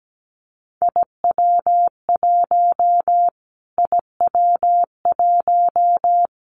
Morse code, heard as a keyed tone, about 17 wpm, 700 Hz. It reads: IW1 IW1
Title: Callsign Prefixes for Top 25 DXCC - Repeated 3x - 17wpm